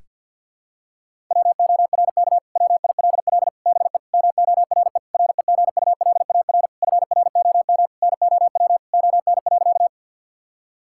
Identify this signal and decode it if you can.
Morse code, heard as a keyed tone, about 50 wpm, 700 Hz. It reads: WORK WILL BE MORE REGULAR FROM NOW ON.
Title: Sentences from Top 1000 Words - Review Part 6 - 50wpm